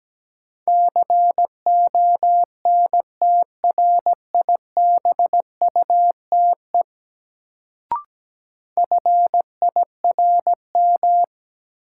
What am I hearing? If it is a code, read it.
Morse code, heard as a keyed tone, about 17 wpm, 700 Hz. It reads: CONTRIBUTE FIRM
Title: Sentences from Top 1400 Words - New Word Drill - 17wpm